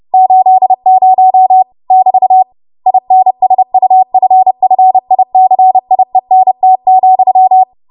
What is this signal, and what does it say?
Morse code, heard as a keyed tone, about 30 wpm, 750 Hz. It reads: 80 <BT> INSUFFICIENT,
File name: Morse / 30WPM